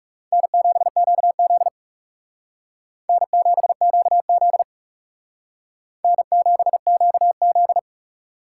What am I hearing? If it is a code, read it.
Morse code, heard as a keyed tone, about 36 wpm, 700 Hz. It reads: N7QZ N7QZ N7QZ
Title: Speed Racing - USA Call Signs 30wpm